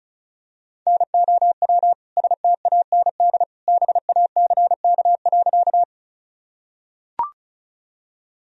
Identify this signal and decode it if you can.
Morse code, heard as a keyed tone, about 36 wpm, 700 Hz. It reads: NOW STAND BACK.
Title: Sentences from Top 1000 Words - Review Part 1 - 35wpm